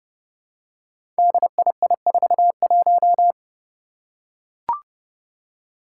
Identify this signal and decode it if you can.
Morse code, heard as a keyed tone, about 28 wpm, 700 Hz. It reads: DII41
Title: Five Challenging Letter-Number Combinations 30wpm